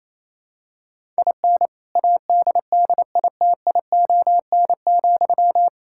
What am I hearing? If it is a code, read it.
Morse code, heard as a keyed tone, about 28 wpm, 700 Hz. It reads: IN ADDITION,
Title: Sentences from Top 1000 Words - Review Part 6 - 28wpm